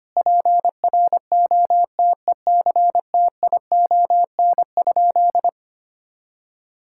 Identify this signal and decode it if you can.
Morse code, heard as a keyed tone, about 25 wpm, 700 Hz. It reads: PROTECTION?
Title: Sentences from Top 1600 Words 25wpm